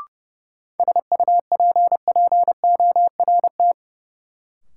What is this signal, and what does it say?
Morse code, heard as a keyed tone, about 30 wpm, 700 Hz. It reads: SUPPORT
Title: Sentences from Top 800 Words - New Word Drill - 30wpm